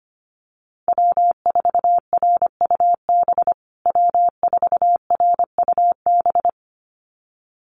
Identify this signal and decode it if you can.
Morse code, heard as a keyed tone, about 25 wpm, 700 Hz. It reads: W4RUB W4RUB
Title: USA Call Signs Repeated 2x 25wpm